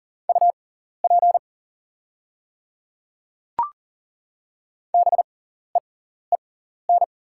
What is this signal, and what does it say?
Morse code, heard as a keyed tone, about 40 wpm, spaced as for 14 wpm, 690 Hz. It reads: UP BEEN
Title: 6x Character Spacing - Top 200 Words - 40wpm